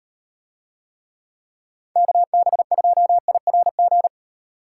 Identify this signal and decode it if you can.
Morse code, heard as a keyed tone, about 36 wpm, 700 Hz. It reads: KB2IRG